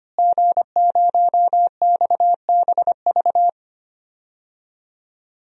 Morse code, heard as a keyed tone, about 25 wpm, 700 Hz. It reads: G0XBV
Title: DXCC-100 Call Signs - 5 Characters - Repeated 3x - 25wpm